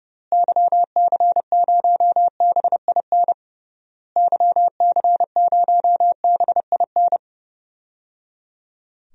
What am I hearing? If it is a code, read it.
Morse code, heard as a keyed tone, about 30 wpm, 700 Hz. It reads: YC0BIN YC0BIN
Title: DXCC-100 Call Signs - 6 Characters - Repeated 3x - 30wpm